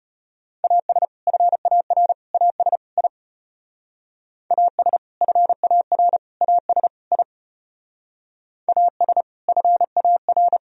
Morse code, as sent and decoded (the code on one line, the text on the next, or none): .- ... / ..-. .- .-. / .- ... / .. / .- ... / ..-. .- .-. / .- ... / .. / .- ... / ..-. .- .-.
AS FAR AS I AS FAR AS I AS FAR